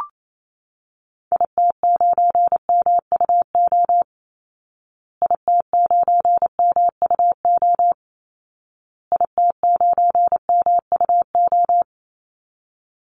Morse code, heard as a keyed tone, about 27 wpm, 700 Hz. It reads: IT9MUO IT9MUO IT9MUO
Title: Call Signs Repeated 2x - 28wpm